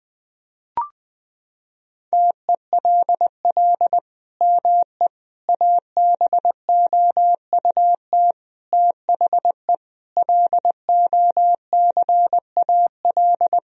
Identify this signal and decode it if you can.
Morse code, heard as a keyed tone, about 20 wpm, 700 Hz. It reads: TELL ME ABOUT THE LOCAL
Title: Sentences from Top 1000 Words - Review Part 3 - 20wpm